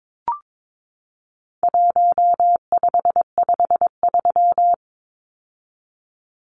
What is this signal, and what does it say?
Morse code, heard as a keyed tone, about 22 wpm, 700 Hz. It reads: 1553
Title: Numbers - 4-Digit 22wpm